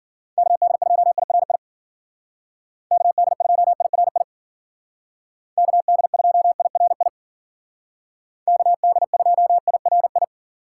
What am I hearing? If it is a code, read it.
Morse code, heard as a keyed tone, about 50 wpm, 700 Hz. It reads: KD2IRI KD2IRI KD2IRI KD2IRI